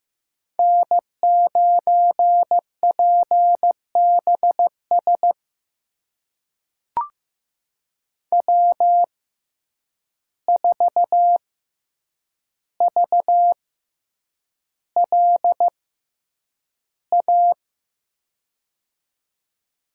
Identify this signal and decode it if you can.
Morse code, heard as a keyed tone, about 15 wpm, spaced as for 8 wpm, 700 Hz. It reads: N9PBS W4VLA
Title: USA Call Signs - 5 Characters - 6x Character Spacing - 15wpm